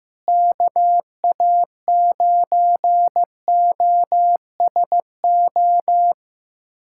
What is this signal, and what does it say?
Morse code, heard as a keyed tone, about 15 wpm, 700 Hz. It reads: KA9OSO
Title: USA Callsign DE Callsign - 6 Characters - 15wpm